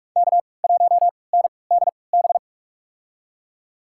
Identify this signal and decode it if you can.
Morse code, heard as a keyed tone, about 46 wpm, spaced as for 26 wpm, 700 Hz. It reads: K1NDB
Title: USA Call Signs - 3x Character Spacing - 45wpm